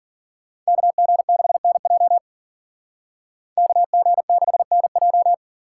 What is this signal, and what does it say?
Morse code, heard as a keyed tone, about 52 wpm, 700 Hz. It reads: KG6NJ KG6NJ